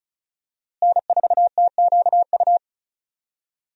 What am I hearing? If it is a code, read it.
Morse code, heard as a keyed tone, about 34 wpm, 700 Hz. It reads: N4TQU
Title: USA Call Signs - 5 Characters - Repeated 2x - 35wpm